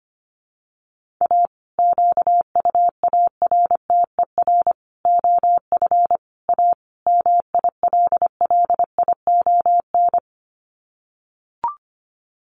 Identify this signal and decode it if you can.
Morse code, heard as a keyed tone, about 25 wpm, 700 Hz. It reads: A QUARTER OF A MILLION
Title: Sets of 5 Words Encore2 25wpm